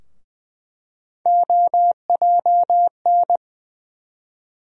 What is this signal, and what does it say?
Morse code, heard as a keyed tone, about 20 wpm, 700 Hz. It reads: OJN
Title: USA Call Sign Postfixes - 3 Letters 20wpm